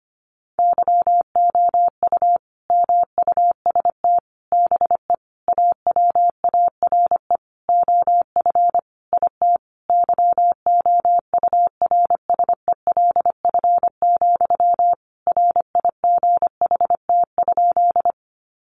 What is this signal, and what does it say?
Morse code, heard as a keyed tone, about 25 wpm, 700 Hz. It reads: YOU MUST BE AWARE OF IT YOURSELF, RIGHT?
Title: Sentences from Top 1200 Words 25wpm